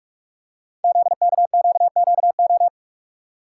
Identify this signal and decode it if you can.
Morse code, heard as a keyed tone, about 44 wpm, 690 Hz. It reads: ZKQQO
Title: Code Groups - 5 Letters - 45wpm